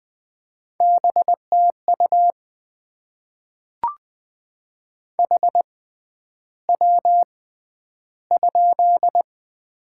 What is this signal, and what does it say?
Morse code, heard as a keyed tone, about 20 wpm, spaced as for 10 wpm, 700 Hz. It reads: BTU HW?
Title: Top 20 QSO Elements - 6x Character Spacing - 20wpm